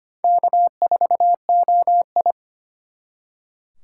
Morse code, heard as a keyed tone, about 32 wpm, 700 Hz. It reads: K4OI